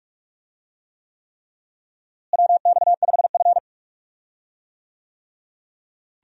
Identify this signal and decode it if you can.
Morse code, heard as a keyed tone, about 42 wpm, 700 Hz. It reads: WX5F
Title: USA Call Signs - 4 Characters - Repeated 3x - 45wpm